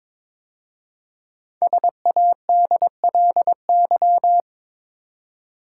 Morse code, heard as a keyed tone, about 22 wpm, 700 Hz. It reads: SADLY
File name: 5 Letter Words - First 3 Letters Given - 22wpm